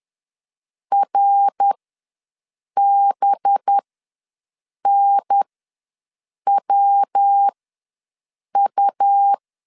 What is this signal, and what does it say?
Morse code, heard as a keyed tone, about 10 wpm, spaced as for 6 wpm, 800 Hz. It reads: RBNWU
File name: LEZIONI / 5 LBVC / 5 LB ws 10wpm 3616